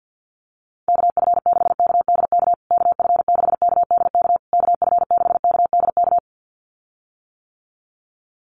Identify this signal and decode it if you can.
Morse code, heard as a keyed tone, about 48 wpm, 700 Hz. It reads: KF6KDK KF6KDK KF6KDK
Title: USA Call Signs Repeated 3x 50wpm